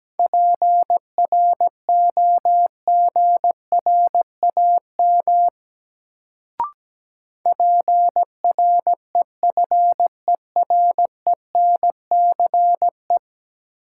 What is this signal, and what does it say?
Morse code, heard as a keyed tone, about 17 wpm, 700 Hz. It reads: PROGRAM PREFERENCE
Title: Sentences from Top 1500 Words - New Word Drill Alt - 17wpm